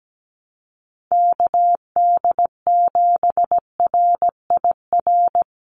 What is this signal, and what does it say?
Morse code, heard as a keyed tone, about 17 wpm, 700 Hz. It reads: KD7RIR
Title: USA Call Signs 17wpm